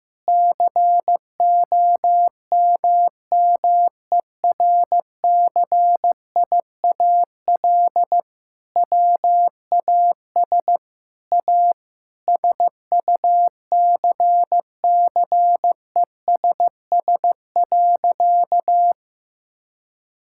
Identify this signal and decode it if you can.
Morse code, heard as a keyed tone, about 15 wpm, 700 Hz. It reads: COMMERCIAL WAS A SUCCESS.
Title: Sentences from Top 1000 Words - Review Part 5 - 15wpm